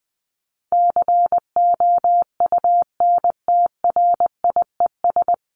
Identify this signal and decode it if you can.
Morse code, heard as a keyed tone, about 20 wpm, 700 Hz. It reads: COUNTRIES
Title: ES:S Suffix - Top 100 Words with Base Words 20wpm